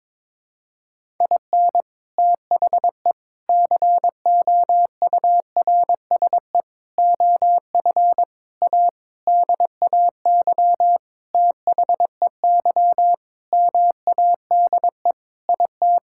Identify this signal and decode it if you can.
Morse code, heard as a keyed tone, about 22 wpm, 700 Hz. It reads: IN THE COURSE OF A DAY THEY MADE IT
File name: practice / 22wpm / Sentences with 4-Word Phrases - 22wpm